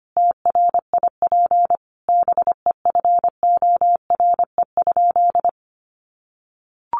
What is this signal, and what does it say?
Morse code, heard as a keyed tone, about 25 wpm, 700 Hz. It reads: TRIP BEFORE?
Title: Sentences from Top 1300 Words 25wpm